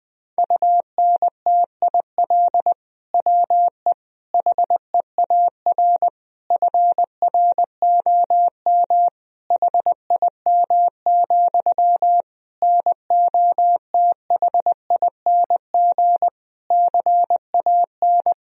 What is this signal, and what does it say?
Morse code, heard as a keyed tone, about 20 wpm, 700 Hz. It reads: UNTIL WE HEAR FROM HIM, NOTHING CAN
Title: Sentences from Top 1000 Words - Review Part 1 - 20wpm